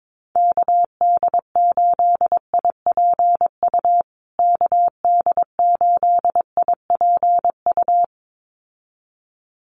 Morse code, heard as a keyed tone, about 22 wpm, 700 Hz. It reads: KD8IPU KD8IPU
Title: USA Call Signs Repeated 2x 22wpm